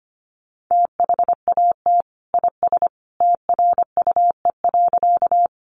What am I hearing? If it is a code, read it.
Morse code, heard as a keyed tone, about 25 wpm, 700 Hz. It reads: THAT IS TRUE.
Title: Sentences from Top 400 Words 25wpm